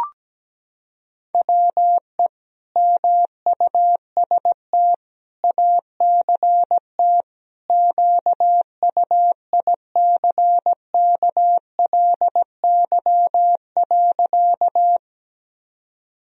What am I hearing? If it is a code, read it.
Morse code, heard as a keyed tone, about 17 wpm, 700 Hz. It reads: WE MUST ACT QUICKLY.
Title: Sentences from Top 1000 Words - Review Part 1 - 17wpm